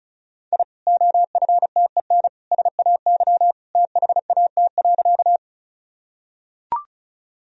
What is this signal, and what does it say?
Morse code, heard as a keyed tone, about 34 wpm, 700 Hz. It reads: I OFTEN SAY THAT.
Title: Sentences from Top 1000 Words - Review Part 1 - 35wpm